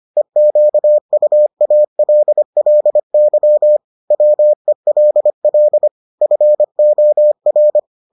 Morse code, heard as a keyed tone, about 25 wpm, 600 Hz. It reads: EQUALLY WELL FOR